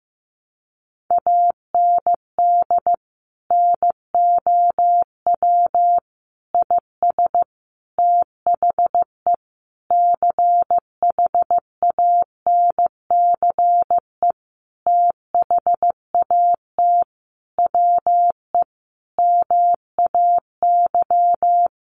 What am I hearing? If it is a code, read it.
Morse code, heard as a keyed tone, about 15 wpm, 700 Hz. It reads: AND NOW IS THE CHANCE THAT WE MAY
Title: Sentences from Top 800 Words 15wpm